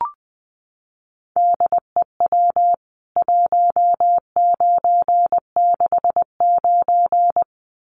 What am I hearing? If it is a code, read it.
Morse code, heard as a keyed tone, about 20 wpm, 700 Hz. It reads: DEW 1969
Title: CWT Exchange 20wpm